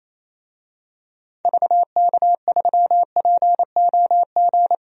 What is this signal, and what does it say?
Morse code, heard as a keyed tone, about 28 wpm, 700 Hz. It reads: VK3POG
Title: DXCC-100 Call Signs - 6 Characters - Repeated 2x - 28wpm